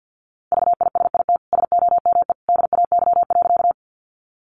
Morse code, heard as a keyed tone, about 50 wpm, 700 Hz. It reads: VISIT SOME DAY.
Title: Sentences from Top 900 Words 50wpm